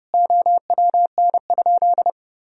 Morse code, heard as a keyed tone, about 30 wpm, 700 Hz. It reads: OWN?
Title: Sentences from Top 1000 Words - Review Part 4 - 30wpm